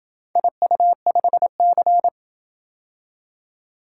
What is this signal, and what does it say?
Morse code, heard as a keyed tone, about 28 wpm, 700 Hz. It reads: IU5C